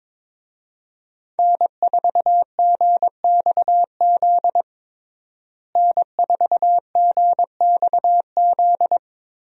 Morse code, heard as a keyed tone, about 22 wpm, 700 Hz. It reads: N4GXZ N4GXZ
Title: USA Call Signs - 5 Characters - Repeated 3x - 22wpm